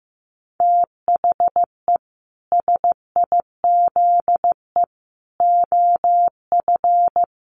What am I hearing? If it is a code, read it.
Morse code, heard as a keyed tone, about 15 wpm, 700 Hz. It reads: THE SIZE OF
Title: Sets of 3 Words 15wpm